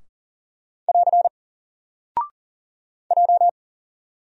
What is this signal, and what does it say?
Morse code, heard as a keyed tone, about 39 wpm, 700 Hz. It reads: <AR> J
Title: Single Character (Letter:Number:Punctuation:Prosign) 40wpm